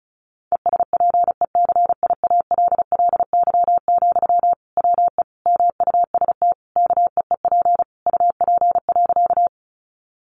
Morse code, heard as a keyed tone, about 34 wpm, 700 Hz. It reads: ESPECIALLY, WE MUST KEEP UP.
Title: Sentences from Top 700 Words 35wpm